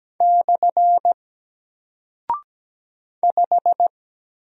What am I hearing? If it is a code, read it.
Morse code, heard as a keyed tone, about 17 wpm, 700 Hz. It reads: / 5
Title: Single Character (Letter:Number:Punctuation:Prosign) Alt 17wpm